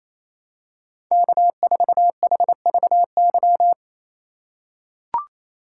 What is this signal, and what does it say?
Morse code, heard as a keyed tone, about 28 wpm, 700 Hz. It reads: K4HVY
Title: USA Call Signs - 5 Characters - 2x Character Spacing - 28wpm